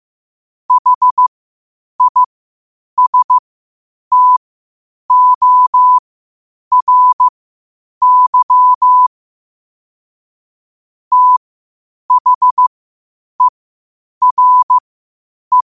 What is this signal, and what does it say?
Morse code, heard as a keyed tone, about 15 wpm, spaced as for 8 wpm, 1000 Hz. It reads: HISTORY THERE
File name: qotdfiles / 8wpm